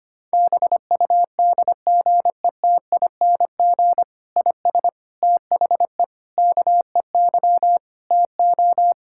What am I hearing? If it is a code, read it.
Morse code, heard as a keyed tone, about 25 wpm, 700 Hz. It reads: BUDGETING IS THE KEY TO
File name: practice / 25wpm / Difficult Sentences with Hints - 25wpm